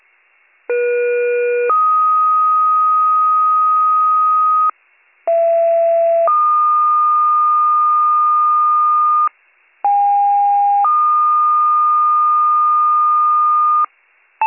Two-tone page, heard as a one-second tone, then a three-second tone, about 480 then 1250 Hz, about 670 then 1220 Hz, about 810 then 1220 Hz. It is heard from 0.7 to 13.9 s.